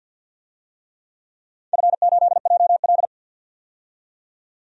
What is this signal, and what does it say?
Morse code, heard as a keyed tone, about 53 wpm, 700 Hz. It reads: F8JL
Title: DXCC-100 Call Signs - 4 Characters - 50wpm